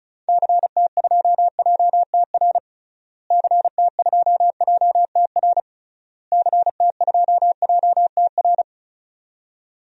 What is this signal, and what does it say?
Morse code, heard as a keyed tone, about 36 wpm, 700 Hz. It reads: CT2JTR CT2JTR CT2JTR
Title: DXCC-100 Call Signs - 6 Characters - Repeated 3x - 35wpm